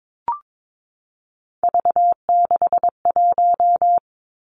Morse code, heard as a keyed tone, about 22 wpm, 700 Hz. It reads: V61
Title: Three Challenging Letter-Number Combinations 22wpm